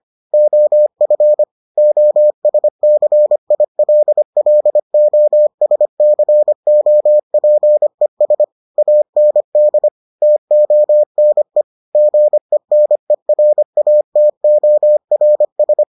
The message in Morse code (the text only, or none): OF OSCILLOSCOPES AND TONE GENERATORS